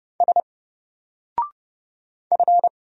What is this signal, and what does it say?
Morse code, heard as a keyed tone, about 31 wpm, 700 Hz. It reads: S F
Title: Single Character (Letter:Number:Punctuation:Prosign) 30wpm